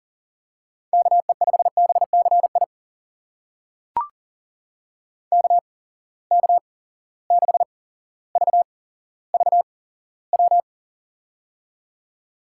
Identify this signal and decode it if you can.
Morse code, heard as a keyed tone, about 39 wpm, spaced as for 17 wpm, 700 Hz. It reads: KE5BCI KK6VVW
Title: USA Call Signs - 6 Characters - 8x Character Spacing - 40wpm